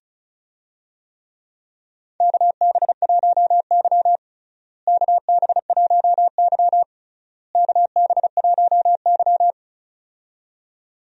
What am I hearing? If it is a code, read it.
Morse code, heard as a keyed tone, about 36 wpm, 700 Hz. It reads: KB1Y KB1Y KB1Y
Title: USA Call Signs - 4 Characters - Repeated 3x - 35wpm